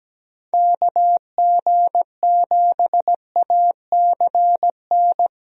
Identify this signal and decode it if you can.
Morse code, heard as a keyed tone, about 17 wpm, 700 Hz. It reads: KG7ACN
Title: USA Call Signs - 6 Characters - 17wpm